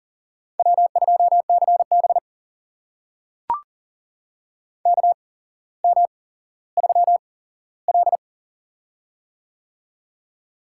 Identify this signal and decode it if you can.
Morse code, heard as a keyed tone, about 40 wpm, spaced as for 17 wpm, 700 Hz. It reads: W2CB KM3L
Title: USA Call Signs - 8x Character Spacing - 40wpm